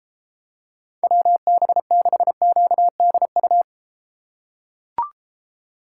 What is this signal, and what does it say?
Morse code, heard as a keyed tone, about 32 wpm, 700 Hz. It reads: WB6QDU